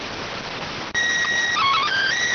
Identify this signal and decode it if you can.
SSTV (slow-scan television), PD 290.